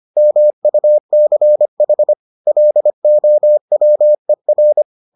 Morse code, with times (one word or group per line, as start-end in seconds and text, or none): 0.2-2.1 MUCH
2.5-4.8 LOWER